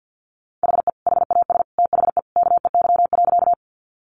Morse code, heard as a keyed tone, about 50 wpm, 700 Hz. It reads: HE HAS THE KEY.
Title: Sentences from Top 600 Words 50wpm